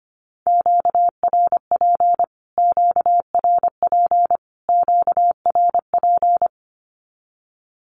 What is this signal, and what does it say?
Morse code, heard as a keyed tone, about 25 wpm, 700 Hz. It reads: QRP QRP QRP
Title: Q-Codes Repeated 3x 25wpm